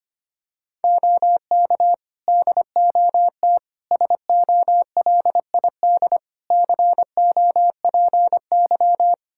Morse code, heard as a keyed tone, about 25 wpm, 700 Hz. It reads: OK DOT SOLID COPY
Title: Practice QSO - Exchange 3 - More Info - 25wpm